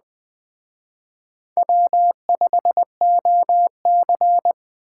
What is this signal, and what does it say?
Morse code, heard as a keyed tone, about 20 wpm, 700 Hz. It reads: W5OC
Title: Speed Racing - USA 1x2 Call Signs 17wpm